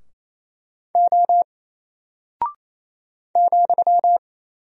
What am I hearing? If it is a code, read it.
Morse code, heard as a keyed tone, about 27 wpm, 700 Hz. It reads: O ,
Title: Single Character (Letter:Number:Punctuation:Prosign) 28wpm